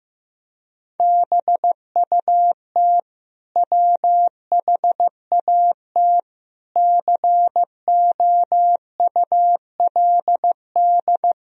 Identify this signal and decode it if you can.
Morse code, heard as a keyed tone, about 15 wpm, 700 Hz. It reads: BUT WHAT COULD
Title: Sentences from Top 1000 Words - Review Part 3 - 15wpm